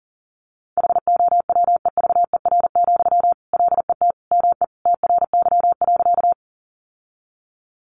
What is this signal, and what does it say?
Morse code, heard as a keyed tone, about 40 wpm, 700 Hz. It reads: HOWEVER, LET ME TRY.
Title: Sentences from Top 600 Words 40wpm